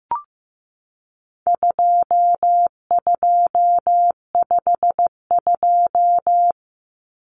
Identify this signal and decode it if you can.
Morse code, heard as a keyed tone, about 15 wpm, 700 Hz. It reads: 2252
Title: Common Cut Numbers - 4-Digit 15wpm